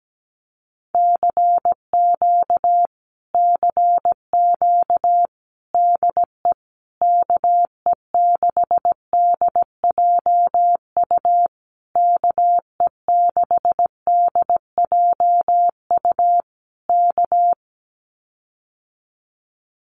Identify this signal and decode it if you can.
Morse code, heard as a keyed tone, about 17 wpm, 700 Hz. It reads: CQ CQ DE KE6DJU KE6DJU K
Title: Common QSO Phrases Encore 17wpm